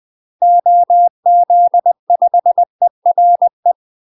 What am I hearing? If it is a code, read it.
Morse code, heard as a keyed tone, about 20 wpm, 700 Hz. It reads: OZ5ERE
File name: UK MASTER QSOs TRAINING / 20WPM / UK MASTER QSOs 0070